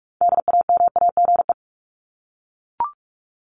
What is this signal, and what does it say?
Morse code, heard as a keyed tone, about 44 wpm, 700 Hz. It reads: DAMAGE
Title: Top 1001-1250 Words 45wpm